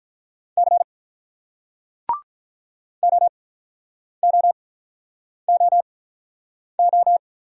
Morse code, heard as a keyed tone, about 47 wpm, spaced as for 9 wpm, 700 Hz. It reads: C OOOO